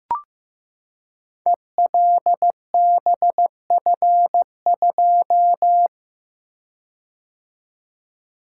Morse code, heard as a keyed tone, about 15 wpm, 700 Hz. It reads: ELBF2
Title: Code Groups - 5 Letter:Numbers - 15wpm